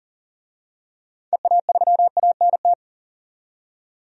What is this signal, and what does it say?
Morse code, heard as a keyed tone, about 40 wpm, 700 Hz. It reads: EA3ANT
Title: DXCC-100 Call Signs - 6 Characters - 40wpm